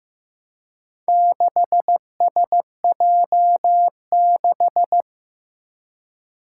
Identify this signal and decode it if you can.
Morse code, heard as a keyed tone, about 15 wpm, 700 Hz. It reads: 6SJ6
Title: Four Challenging Letter-Number Combinations 15wpm